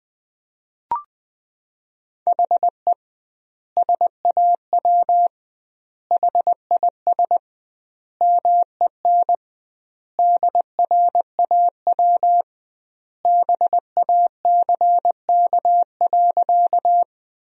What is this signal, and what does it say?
Morse code, heard as a keyed tone, about 21 wpm, 700 Hz. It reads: HE SAW HIS MEN DRAW BACK.
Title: 2x Word-Spacing - Sentences from Top 300 Words - 20wpm